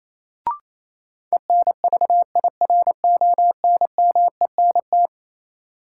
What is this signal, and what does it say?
Morse code, heard as a keyed tone, about 28 wpm, 700 Hz. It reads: ENVIRONMENT
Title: Sentences from Top 1000 Words - Review Part 5 - New Word Drill - 28wpm